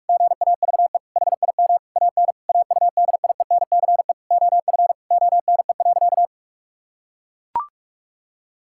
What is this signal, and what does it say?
Morse code, heard as a keyed tone, about 42 wpm, 700 Hz. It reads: GAVE HIM AN AUDIENCE OF ONE.
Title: Sentences from Top 1000 Words - Review Part 5 - 45wpm